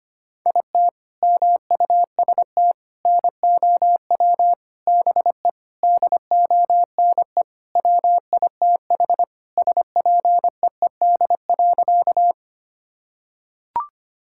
Morse code, heard as a keyed tone, about 25 wpm, 700 Hz. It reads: IT MUST NOW BE DONE WITH SPEED.
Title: Sentences from Top 1000 Words - Review Part 5 - 25wpm